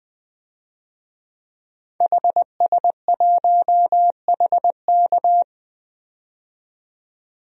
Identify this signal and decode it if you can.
Morse code, heard as a keyed tone, about 20 wpm, 700 Hz. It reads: HS1HK
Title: DXCC-100 Call Signs - 5 Characters - 20wpm